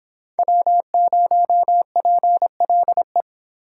Morse code, heard as a keyed tone, about 25 wpm, 700 Hz. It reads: W0PLE